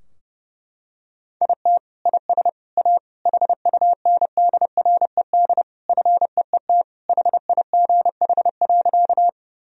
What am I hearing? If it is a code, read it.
Morse code, heard as a keyed tone, about 30 wpm, 700 Hz. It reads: IT IS A HUNDRED FEET HIGH.
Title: Sentences from Top 1000 Words - Review Part 2 - 30wpm